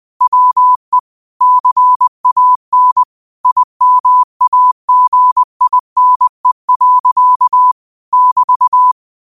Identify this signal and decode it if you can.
Morse code, heard as a keyed tone, about 20 wpm, 1000 Hz. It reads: WE CAN IMAGINE. <BT>
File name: qotdfiles / 20wpm